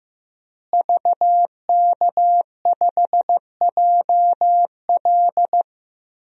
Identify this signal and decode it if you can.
Morse code, heard as a keyed tone, about 15 wpm, 700 Hz. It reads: VK5JL
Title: DXCC-100 Call Signs - 5 Characters - Repeated 3x - 15wpm